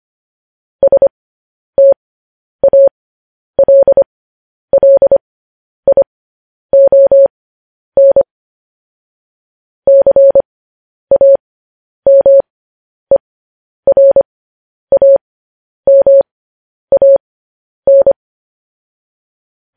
Morse code, heard as a keyed tone, about 25 wpm, spaced as for 10 wpm, 550 Hz. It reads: STALLION CAMERAMAN
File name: podcast / News headlines-1x-25x10